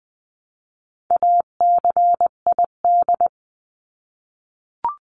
Morse code, heard as a keyed tone, about 20 wpm, 700 Hz. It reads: ACID
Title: 259 Lesson 42d - Words 18-20wpm